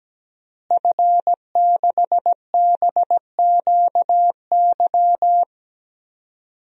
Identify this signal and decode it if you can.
Morse code, heard as a keyed tone, about 17 wpm, 700 Hz. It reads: F6BQY
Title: DXCC-100 Call Signs - 5 Characters - Repeated 3x - 17wpm